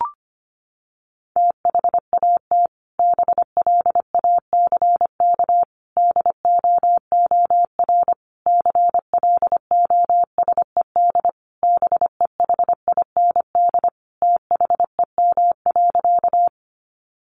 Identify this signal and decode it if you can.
Morse code, heard as a keyed tone, about 25 wpm, 700 Hz. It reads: THAT BLACK DOOR CLOSED BEHIND THEM.
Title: Sentences from Top 1400 Words 25wpm